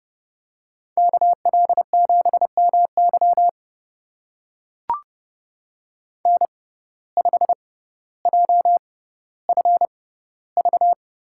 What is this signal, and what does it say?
Morse code, heard as a keyed tone, about 32 wpm, spaced as for 16 wpm, 700 Hz. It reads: KL7MY N5JFV
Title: USA Call Signs - 5 Characters - 6x Character Spacing - 30wpm